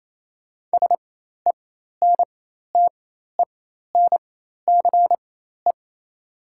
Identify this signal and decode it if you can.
Morse code, heard as a keyed tone, about 28 wpm, spaced as for 13 wpm, 700 Hz. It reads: SENTENCE
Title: 4x Character Spacing - Top 200 Words - 28wpm